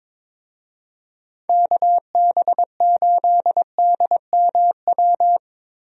Morse code, heard as a keyed tone, about 21 wpm, 700 Hz. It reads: KB8DMW
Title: USA Call Signs - 6 Characters - Repeated 2x - 22wpm